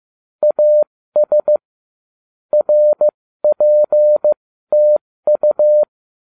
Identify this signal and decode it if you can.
Morse code, heard as a keyed tone, about 15 wpm, spaced as for 13 wpm, 600 Hz. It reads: AS RPTU